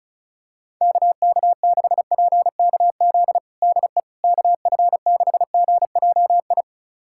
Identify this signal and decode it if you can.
Morse code, heard as a keyed tone, about 36 wpm, 700 Hz. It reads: KK6PKZ DE KF6GJI
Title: USA Callsign DE Callsign 35wpm